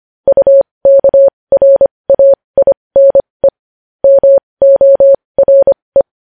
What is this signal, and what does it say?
Morse code, heard as a keyed tone, about 25 wpm, spaced as for 20 wpm, 550 Hz. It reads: UKRAINE MORE
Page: news-headlines-in-morse-code-at-20-wpm